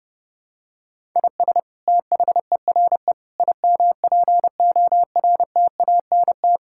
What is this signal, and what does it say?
Morse code, heard as a keyed tone, about 32 wpm, 700 Hz. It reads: IS THERE IMPORTANT
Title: Sentences from Top 1000 Words - Review Part 6 - 30wpm